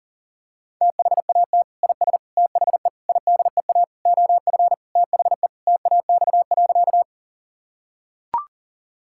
Morse code, heard as a keyed tone, about 40 wpm, 700 Hz. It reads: THAT IS THE IDEA OF THE TAX.
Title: Sentences from Top 1000 Words - Review Part 5 - 40wpm